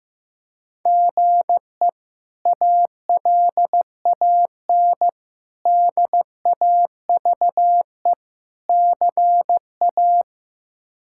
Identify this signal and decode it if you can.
Morse code, heard as a keyed tone, about 15 wpm, 700 Hz. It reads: GE ALAN DAVE CA
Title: K1USN SST QSO Practice - 15wpm